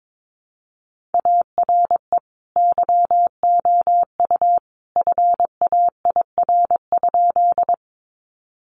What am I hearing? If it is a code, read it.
Morse code, heard as a keyed tone, about 22 wpm, 700 Hz. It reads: ARE YOU FAIR?
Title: Sentences from Top 1300 Words 22wpm